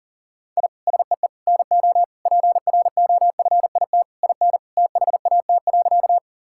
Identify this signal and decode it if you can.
Morse code, heard as a keyed tone, about 40 wpm, 700 Hz. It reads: I SEE NO PROFIT IN THAT.
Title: Sentences from Top 1000 Words - Review Part 5 - 40wpm